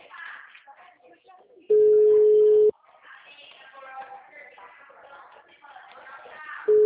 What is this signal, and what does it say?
Ringback tone (Europe, 425 Hz), with speech.